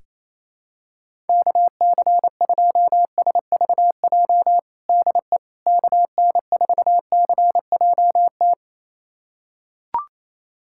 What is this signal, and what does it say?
Morse code, heard as a keyed tone, about 28 wpm, 700 Hz. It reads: KC2SVJ DE KN4CJT
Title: USA Callsign DE Callsign - 6 Characters - 28wpm